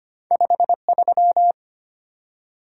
Morse code, heard as a keyed tone, about 25 wpm, 700 Hz. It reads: 53
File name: Two Letter-Number Combinations - 25wpm